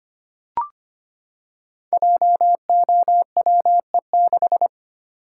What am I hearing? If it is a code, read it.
Morse code, heard as a keyed tone, about 25 wpm, 700 Hz. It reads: JOWE6
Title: Code Groups - 5 Letter:Numbers - 25wpm